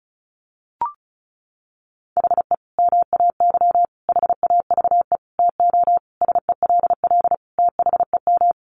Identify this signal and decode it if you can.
Morse code, heard as a keyed tone, about 34 wpm, 700 Hz. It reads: HE MAY HAVE TO SELL THEM
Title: Sentences from Top 1100 Words 35wpm